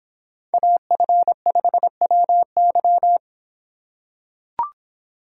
Morse code, heard as a keyed tone, about 26 wpm, 700 Hz. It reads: AF5WY